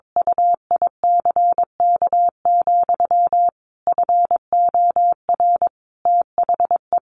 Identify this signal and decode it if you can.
Morse code, heard as a keyed tone, about 22 wpm, 700 Hz. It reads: UICK, FOR THE
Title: Sentences from Top 500 Words 22wpm